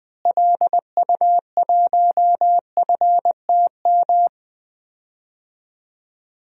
Morse code, heard as a keyed tone, about 20 wpm, 700 Hz. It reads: LU1FTM
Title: DXCC-100 Call Signs - 6 Characters - Repeated 3x - 20wpm